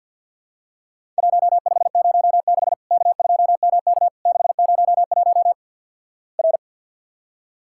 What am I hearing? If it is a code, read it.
Morse code, heard as a keyed tone, about 50 wpm, 700 Hz. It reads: 1506 K2MK 601 R
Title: Worked All Europe DX Contest - QTC Exchange - 50wpm